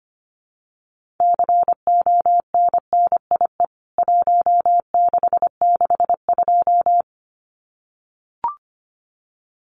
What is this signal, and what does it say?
Morse code, heard as a keyed tone, about 25 wpm, 700 Hz. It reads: CONNIE 1662
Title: CWT Exchange 25wpm